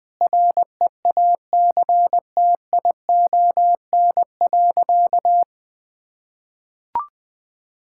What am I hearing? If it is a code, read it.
Morse code, heard as a keyed tone, about 20 wpm, 700 Hz. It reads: REACTION.
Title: Sentences with 8-Word Phrases - 20wpm